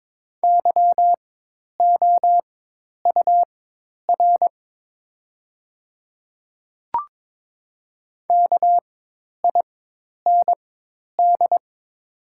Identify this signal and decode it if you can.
Morse code, heard as a keyed tone, about 22 wpm, spaced as for 10 wpm, 690 Hz. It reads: YOUR KIND
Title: 4x Character Spacing - Top 200 Words - 22wpm